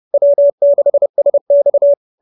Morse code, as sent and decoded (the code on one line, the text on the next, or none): .-- -.... ... -..-
W6SX